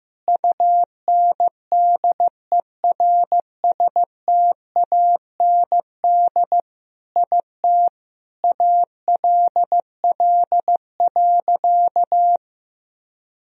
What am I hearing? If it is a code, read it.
Morse code, heard as a keyed tone, about 15 wpm, 700 Hz. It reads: UNDERSTAND IT ALL.
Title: Sentences from Top 1000 Words - Review Part 4 - 15wpm